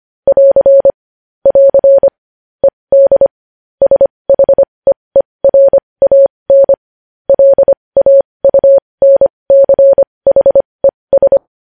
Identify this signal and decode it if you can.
Morse code, heard as a keyed tone, about 25 wpm, spaced as for 20 wpm, 550 Hz. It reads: <AR> <AR> ED SHEERAN LAUNCHES
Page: News headlines in morse code at 20 wpm